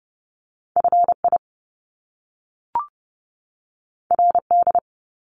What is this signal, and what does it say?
Morse code, heard as a keyed tone, about 30 wpm, 700 Hz. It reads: FI RD
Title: Top Two Letter Combinations in Words 30wpm